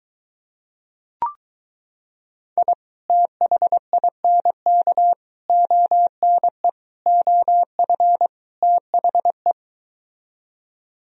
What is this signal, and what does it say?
Morse code, heard as a keyed tone, about 23 wpm, 700 Hz. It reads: I THINK ONE OF THE